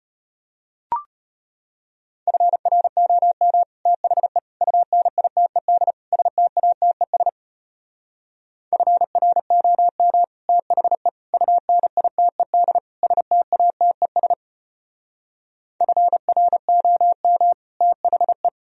Morse code, heard as a keyed tone, about 33 wpm, 700 Hz. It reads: FROM THE UNITED STATES FROM THE UNITED STATES FROM THE